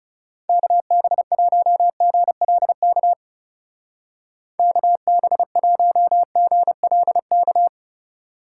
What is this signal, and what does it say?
Morse code, heard as a keyed tone, about 37 wpm, 700 Hz. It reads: KB1GLK KB1GLK
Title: Speed Racing - USA 2x3 Call Signs 30wpm